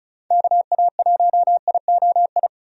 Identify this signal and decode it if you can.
Morse code, heard as a keyed tone, about 36 wpm, 700 Hz. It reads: KA1IOI
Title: USA Call Signs - 6 Characters - 35wpm